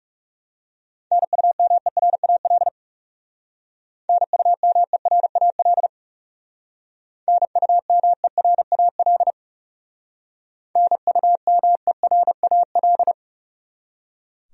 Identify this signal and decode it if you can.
Morse code, heard as a keyed tone, about 37 wpm, 700 Hz. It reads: NUMERAL NUMERAL NUMERAL NUMERAL
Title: Speed-Racing Top 500 Words 30wpm